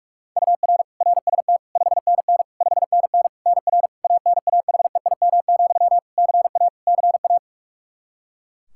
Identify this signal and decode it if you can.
Morse code, heard as a keyed tone, about 46 wpm, 700 Hz. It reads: UR RST 5NN 5NN NR ANAHEIM, CA CA
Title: Practice QSO - Exchange 1 - Beginning - 45wpm